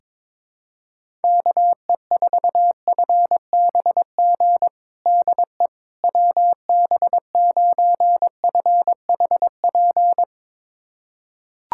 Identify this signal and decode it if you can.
Morse code, heard as a keyed tone, about 22 wpm, 700 Hz. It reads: KE4FBG DE WB9FHP
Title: USA Callsign DE Callsign - 6 Characters - 22wpm